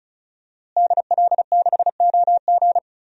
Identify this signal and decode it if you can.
Morse code, heard as a keyed tone, about 34 wpm, 700 Hz. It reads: DL6OG
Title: DXCC-100 Call Signs - 5 Characters - Repeated 2x - 35wpm